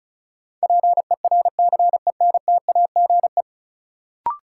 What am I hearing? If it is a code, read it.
Morse code, heard as a keyed tone, about 36 wpm, 700 Hz. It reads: PERCENTAGE
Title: Sentences from Top 1300 Words - New Word Drill - 35wpm